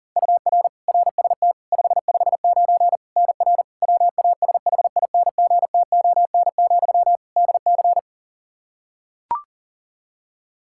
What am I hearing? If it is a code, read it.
Morse code, heard as a keyed tone, about 40 wpm, 700 Hz. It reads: UR RST 559 NR WASHINGTON, DC
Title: K1USN SST QSO Practice 40wpm